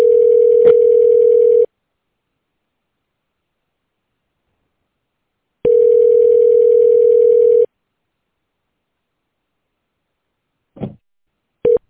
Ringback tone (North America), 440 + 480 Hz.